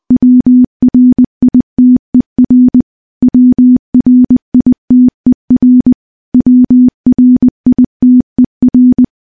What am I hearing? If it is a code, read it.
Morse code, heard as a keyed tone, about 20 wpm, 250 Hz. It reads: WRITER WRITER WRITER
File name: morse-training-20WPM